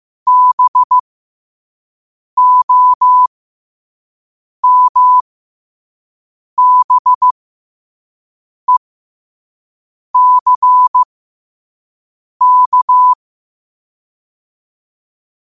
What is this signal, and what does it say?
Morse code, heard as a keyed tone, about 15 wpm, spaced as for 5 wpm, 1000 Hz. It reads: BOMBECK
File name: qotdfiles / 5wpm